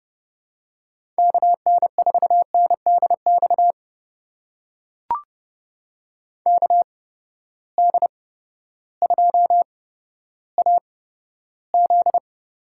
Morse code, heard as a keyed tone, about 30 wpm, spaced as for 14 wpm, 700 Hz. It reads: KN4NDX KD2AZ
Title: USA Call Signs - 6 Characters - 8x Character Spacing - 30wpm